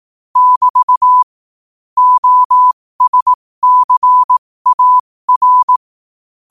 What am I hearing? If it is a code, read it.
Morse code, heard as a keyed tone, about 18 wpm, spaced as for 15 wpm, 1000 Hz. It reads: <BT> OSCAR
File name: qotdfiles / 15wpm